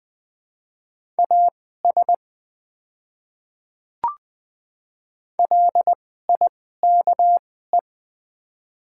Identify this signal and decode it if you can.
Morse code, heard as a keyed tone, about 20 wpm, spaced as for 15 wpm, 700 Hz. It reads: AS LIKE
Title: 2x Character Spacing - Top 100 Words - 20wpm